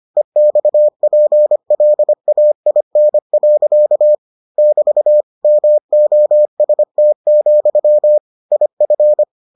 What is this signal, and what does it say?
Morse code, heard as a keyed tone, about 25 wpm, 600 Hz. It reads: EXPLAIN. <BT> MOST, IF